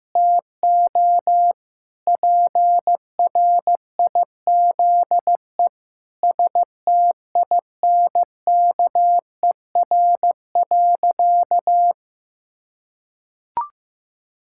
Morse code, heard as a keyed tone, about 16 wpm, 700 Hz. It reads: TO PRIZE STINKER.
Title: Pangram Challenge v2 - 15wpm